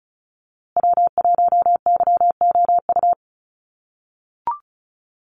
Morse code, heard as a keyed tone, about 34 wpm, 700 Hz. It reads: W1YOU
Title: USA Call Signs Repeated 3x 35wpm